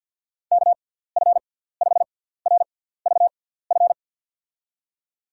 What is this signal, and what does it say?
Morse code, heard as a keyed tone, about 51 wpm, spaced as for 17 wpm, 700 Hz. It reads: KF5RVF
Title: USA Call Signs - 6 Characters - 6x Character Spacing - 50wpm